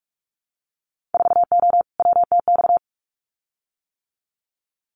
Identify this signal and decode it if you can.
Morse code, heard as a keyed tone, about 44 wpm, 700 Hz. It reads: 4O WTX
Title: Winter Field Day Exchanges Encore 45wpm